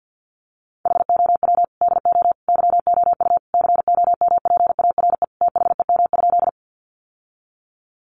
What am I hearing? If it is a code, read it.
Morse code, heard as a keyed tone, about 50 wpm, 700 Hz. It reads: HOW DO YOU COMPARE THEM?
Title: Sentences from Top 1300 Words 50wpm